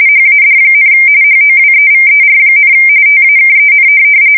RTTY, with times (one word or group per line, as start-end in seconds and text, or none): 0.1-0.9 HAD,
1.2-2.7 HOWEVER,
3.1-3.9 TAKEN